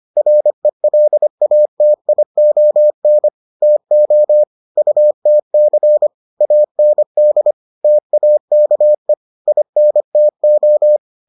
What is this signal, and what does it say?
Morse code, heard as a keyed tone, about 25 wpm, 600 Hz. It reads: RELATION TO UTC AND TAKE INTO